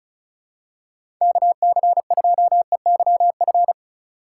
Morse code, heard as a keyed tone, about 36 wpm, 700 Hz. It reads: KC2EYF
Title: USA Call Signs - 6 Characters - 3x Character Spacing - 35wpm